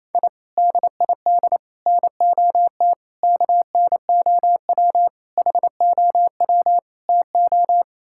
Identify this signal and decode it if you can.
Morse code, heard as a keyed tone, about 28 wpm, 700 Hz. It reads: I DID NOT KNOW HOW TO
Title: Sentences from Top 1000 Words - Review Part 6 - 28wpm